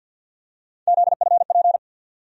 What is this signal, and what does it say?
Morse code, heard as a keyed tone, about 51 wpm, 700 Hz. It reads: ZFP
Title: USA Call Sign Postfixes - 3 Letters 50wpm